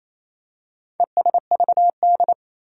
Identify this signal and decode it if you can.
Morse code, heard as a keyed tone, about 27 wpm, 700 Hz. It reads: ESVD
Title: Four Challenging Letter-Number Combinations 28wpm